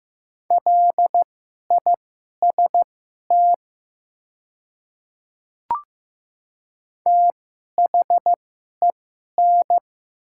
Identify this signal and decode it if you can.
Morse code, heard as a keyed tone, about 15 wpm, spaced as for 11 wpm, 700 Hz. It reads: LIST THEN
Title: 2x Character Spacing - Top 500 Words - 15wpm